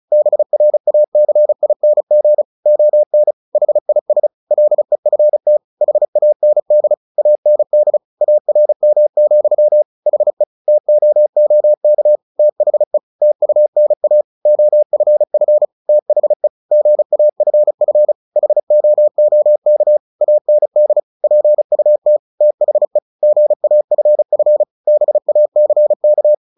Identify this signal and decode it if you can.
Morse code, heard as a keyed tone, about 34 wpm, 600 Hz. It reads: BRACING ON HIS LEFT HAND AND ARM, HE TOOK THE TUNA OFF THE GAFF HOOK AND PUT THE GAFF BACK
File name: the old man and the sea / the old man and the sea 1 35wpm